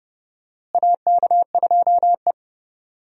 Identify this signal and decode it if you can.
Morse code, heard as a keyed tone, about 30 wpm, 700 Hz. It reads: AK2E
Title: USA Call Signs - 4 Characters - Repeated 2x - 30wpm